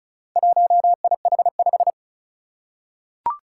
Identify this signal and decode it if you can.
Morse code, heard as a keyed tone, about 36 wpm, 700 Hz. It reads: 1IH5
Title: Four Challenging Letter-Number Combinations 35wpm